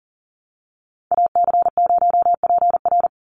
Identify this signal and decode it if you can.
Morse code, heard as a keyed tone, about 40 wpm, 700 Hz. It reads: AC0PR
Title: NA SOTA Activator Callsigns 40wpm